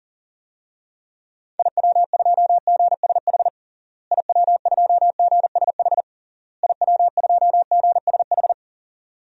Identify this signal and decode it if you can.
Morse code, heard as a keyed tone, about 40 wpm, 700 Hz. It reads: IW2GSH IW2GSH IW2GSH
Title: DXCC-100 Call Signs - 6 Characters - Repeated 3x - 40wpm